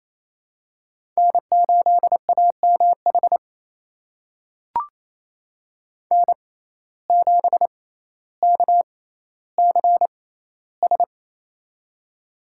Morse code, heard as a keyed tone, about 28 wpm, spaced as for 14 wpm, 700 Hz. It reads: N8AMH N7KCS
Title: USA Call Signs - 5 Characters - 6x Character Spacing - 28wpm